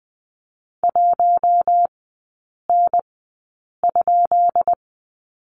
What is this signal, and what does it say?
Morse code, heard as a keyed tone, about 20 wpm, spaced as for 8 wpm, 700 Hz. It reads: 1N?
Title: 133 Lesson 22e - Question Mark - Three Characters 20wpm